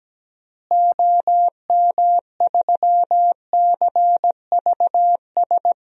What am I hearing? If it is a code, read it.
Morse code, heard as a keyed tone, about 17 wpm, 700 Hz. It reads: OM3CVS
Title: DXCC-100 Call Signs - 6 Characters - Repeated 3x - 17wpm